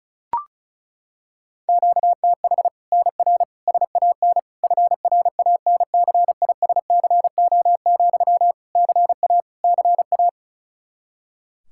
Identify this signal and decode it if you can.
Morse code, heard as a keyed tone, about 36 wpm, 700 Hz. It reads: QTH NR SAN FRANCISCO, CA CA
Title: Practice QSO - Exchange 1 - Beginning - 35wpm